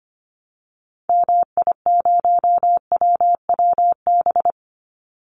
Morse code, heard as a keyed tone, about 25 wpm, 700 Hz. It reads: MI0WWB
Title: Call Signs Repeated 2x - 25wpm